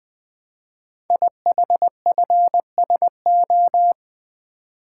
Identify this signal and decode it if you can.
Morse code, heard as a keyed tone, about 20 wpm, 700 Hz. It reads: IHFSO
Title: Code Groups - 5 Letters - 20wpm